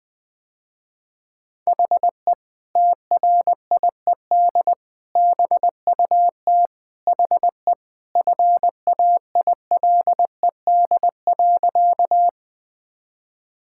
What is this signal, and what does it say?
Morse code, heard as a keyed tone, about 21 wpm, 700 Hz. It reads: HE TRIED BUT HE FAILED.
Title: Sentences with 2-Word Phrases - Spoken Beginning - 20wpm